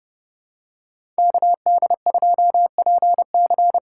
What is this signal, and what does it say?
Morse code, heard as a keyed tone, about 30 wpm, 700 Hz. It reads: KD2PC
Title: USA Call Signs - 6 Characters - 8x Character Spacing - 30wpm